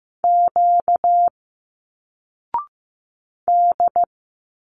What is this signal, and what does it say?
Morse code, heard as a keyed tone, about 15 wpm, 700 Hz. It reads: Q D
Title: Single Letters 15wpm